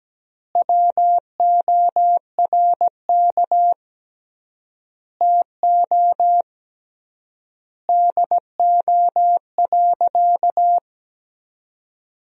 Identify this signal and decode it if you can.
Morse code, heard as a keyed tone, about 17 wpm, 700 Hz. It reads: WORK TO DO.
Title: 3x Word-Spacing - Sentences from Top 200 Words - 17wpm